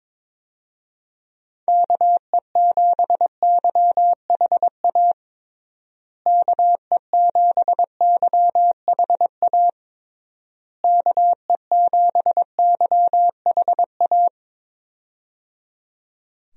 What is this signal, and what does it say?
Morse code, heard as a keyed tone, about 22 wpm, 700 Hz. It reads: KE7YHA KE7YHA KE7YHA
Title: USA Call Signs - 6 Characters - Repeated 3x - 22wpm